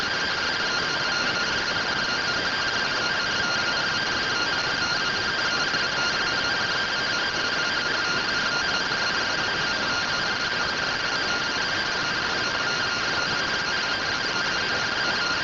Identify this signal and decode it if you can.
Two-tone teleprinter signal, about 170 Hz shift.